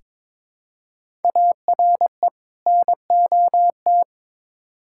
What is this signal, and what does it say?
Morse code, heard as a keyed tone, about 22 wpm, 700 Hz. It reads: ARE NOT
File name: Build A Phrase - 2-Word Phrases - 22wpm